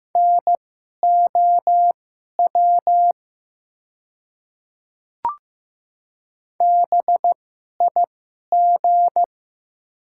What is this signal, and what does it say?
Morse code, heard as a keyed tone, about 15 wpm, spaced as for 11 wpm, 700 Hz. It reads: NOW BIG
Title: 2x Character Spacing - Top 300 Words - 15wpm